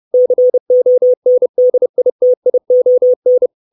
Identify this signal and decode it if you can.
Morse code, heard as a keyed tone, about 30 wpm, 500 Hz. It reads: CONDITION
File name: 632 Words 30WPM